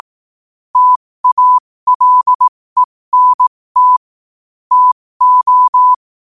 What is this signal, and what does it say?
Morse code, heard as a keyed tone, about 18 wpm, spaced as for 15 wpm, 1000 Hz. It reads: TALENT TO
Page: New quotes every day in morse code at 15 Words per minute.